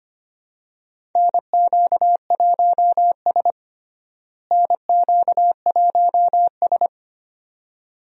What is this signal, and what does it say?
Morse code, heard as a keyed tone, about 25 wpm, 700 Hz. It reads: NQ1S NQ1S
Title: USA Call Signs - 4 Characters - Repeated 2x - 25wpm